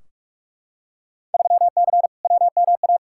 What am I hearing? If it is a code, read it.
Morse code, heard as a keyed tone, about 42 wpm, 700 Hz. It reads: 3C WMA
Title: Field Day Exchanges Encore - 45wpm